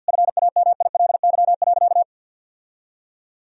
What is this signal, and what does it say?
Morse code, heard as a keyed tone, about 52 wpm, 700 Hz. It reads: FAMILY.
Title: Sentences from Top 1000 Words - Review Part 2 - 50wpm